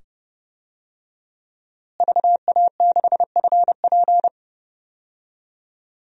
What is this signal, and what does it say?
Morse code, heard as a keyed tone, about 30 wpm, 700 Hz. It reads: VA6FP
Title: DXCC-100 Call Signs - 5 Characters - Repeated 2x - 30wpm